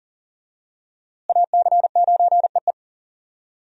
Morse code, heard as a keyed tone, about 37 wpm, 690 Hz. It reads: AC9EE
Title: USA Call Signs - 5 Characters - Repeated 3x - 40wpm